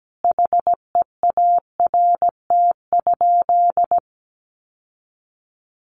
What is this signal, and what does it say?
Morse code, heard as a keyed tone, about 17 wpm, 700 Hz. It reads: HEART?
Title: Sentences from Top 1200 Words 17wpm